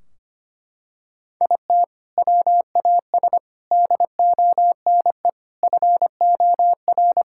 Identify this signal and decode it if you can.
Morse code, heard as a keyed tone, about 25 wpm, 700 Hz. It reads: IT WAS DONE FOR
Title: Sentences from Top 1600 Words 25wpm